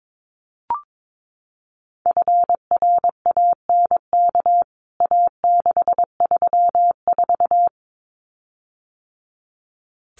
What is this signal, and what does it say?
Morse code, heard as a keyed tone, about 22 wpm, 700 Hz. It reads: FRANK A634
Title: CWT Exchange 22wpm